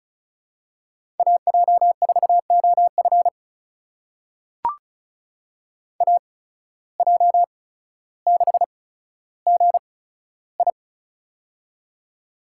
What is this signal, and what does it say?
Morse code, heard as a keyed tone, about 35 wpm, spaced as for 15 wpm, 690 Hz. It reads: AJ4OF AJ6GI
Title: USA Call Signs - 5 Characters - 8x Character Spacing - 35wpm